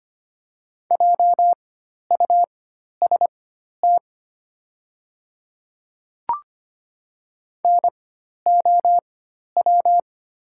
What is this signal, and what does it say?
Morse code, heard as a keyed tone, about 25 wpm, spaced as for 12 wpm, 700 Hz. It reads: JUST NOW
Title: 4x Character Spacing - Top 500 Words - 25wpm